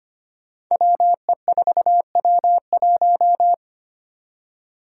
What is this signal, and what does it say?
Morse code, heard as a keyed tone, about 25 wpm, 700 Hz. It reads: WE4W1
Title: Five Challenging Letter-Number Combinations 25wpm